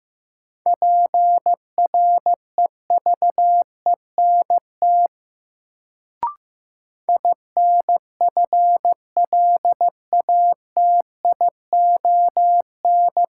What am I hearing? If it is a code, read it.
Morse code, heard as a keyed tone, about 15 wpm, 700 Hz. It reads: PREVENT INFLATION
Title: Sentences from Top 1100 Words - New Word Drill - 15wpm